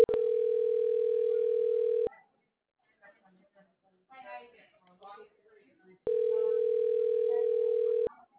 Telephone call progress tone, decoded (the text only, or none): ringback tone